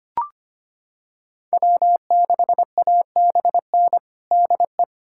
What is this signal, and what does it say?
Morse code, heard as a keyed tone, about 25 wpm, 700 Hz. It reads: W6ABN DE
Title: USA Callsign DE Callsign 25wpm